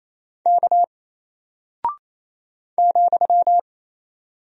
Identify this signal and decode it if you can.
Morse code, heard as a keyed tone, about 27 wpm, 700 Hz. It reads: K ,
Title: Single Character (Letter:Number:Punctuation:Prosign) Alt 28wpm